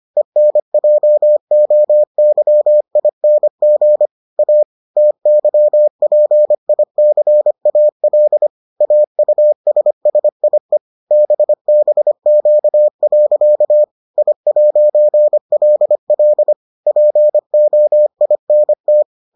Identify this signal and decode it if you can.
Morse code, heard as a keyed tone, about 25 wpm, 600 Hz. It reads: ENJOYING A TYPICAL AUSSIE BBQ. I'LL POINT